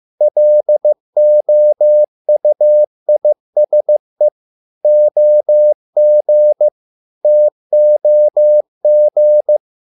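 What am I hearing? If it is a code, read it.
Morse code, heard as a keyed tone, about 15 wpm, 600 Hz. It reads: LOUISE OG TOG
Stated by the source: Morse code at a slow 15 wpm from CW med Gnister.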